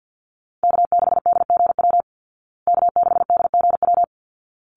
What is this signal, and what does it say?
Morse code, heard as a keyed tone, about 50 wpm, 700 Hz. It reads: K6DGW K6DGW
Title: Call Signs Repeated 3x - 50wpm